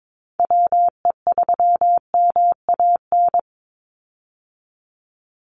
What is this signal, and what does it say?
Morse code, heard as a keyed tone, about 22 wpm, 700 Hz. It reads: WE3MAN
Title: USA Call Signs 22wpm